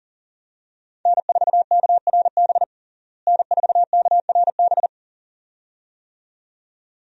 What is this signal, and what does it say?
Morse code, heard as a keyed tone, about 37 wpm, 700 Hz. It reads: N4KRB N4KRB
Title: USA Call Signs - 5 Characters - Repeated 2x - 40wpm